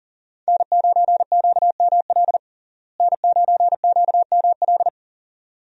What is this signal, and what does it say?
Morse code, heard as a keyed tone, about 40 wpm, 700 Hz. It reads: N9QML N9QML
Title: USA Call Signs - 5 Characters - Repeated 2x - 40wpm